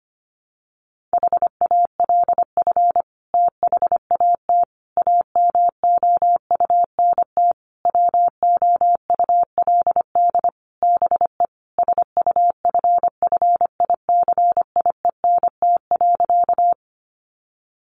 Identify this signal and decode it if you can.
Morse code, heard as a keyed tone, about 25 wpm, 700 Hz. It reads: HALF THAT AMOUNT WOULD BE SUFFICIENT.
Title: Sentences from Top 1400 Words 25wpm